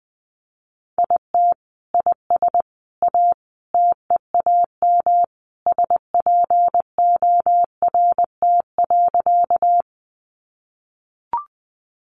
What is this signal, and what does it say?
Morse code, heard as a keyed tone, about 20 wpm, 700 Hz. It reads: IT IS A TEAM SPORT.
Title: Sentences from Top 1500 Words 20wpm